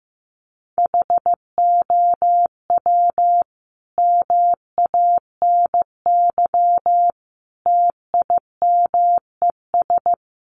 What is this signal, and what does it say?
Morse code, heard as a keyed tone, about 15 wpm, 700 Hz. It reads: HOW MANY TIMES
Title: A Sound Of Thunder 15wpm